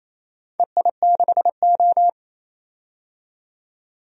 Morse code, heard as a keyed tone, about 28 wpm, 700 Hz. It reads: EI6O
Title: DXCC-100 Call Signs - 4 Characters - 28wpm